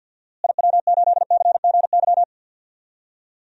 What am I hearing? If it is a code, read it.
Morse code, heard as a keyed tone, about 52 wpm, 700 Hz. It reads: IW8CGY
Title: DXCC-100 Call Signs - 6 Characters - Repeated 3x - 50wpm